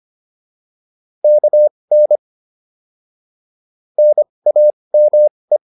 Morse code, heard as a keyed tone, about 25 wpm, spaced as for 20 wpm, 600 Hz. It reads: KN NAME